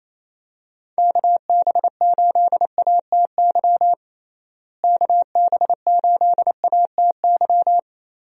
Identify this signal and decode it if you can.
Morse code, heard as a keyed tone, about 28 wpm, 700 Hz. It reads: KB8ATY KB8ATY
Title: USA Call Signs - 6 Characters - Repeated 2x - 28wpm